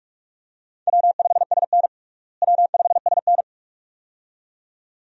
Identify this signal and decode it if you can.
Morse code, heard as a keyed tone, about 48 wpm, 700 Hz. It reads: W5SN W5SN
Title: USA Call Signs - 4 Characters - Repeated 2x - 45wpm